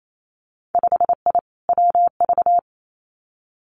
Morse code, heard as a keyed tone, about 28 wpm, 700 Hz. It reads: 5I WV
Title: Winter Field Day Exchanges Encore 28wpm